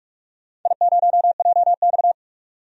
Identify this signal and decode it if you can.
Morse code, heard as a keyed tone, about 46 wpm, 700 Hz. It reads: I0JX
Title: DXCC-100 Call Signs - 4 Characters - 45wpm